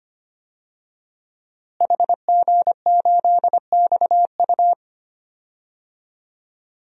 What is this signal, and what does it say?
Morse code, heard as a keyed tone, about 25 wpm, 700 Hz. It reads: HG8XU
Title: DXCC-100 Call Signs - 5 Characters - Repeated 3x - 25wpm